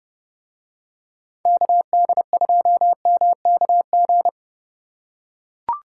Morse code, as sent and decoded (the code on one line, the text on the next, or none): -.- -.. ..--- -- -.- --.
KD2MKG